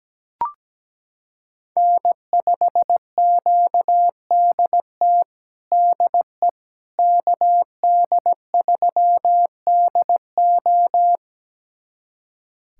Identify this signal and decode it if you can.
Morse code, heard as a keyed tone, about 17 wpm, 700 Hz. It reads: N5QDT DE KD3DO
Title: USA Callsign DE Callsign - 5 Characters - 17wpm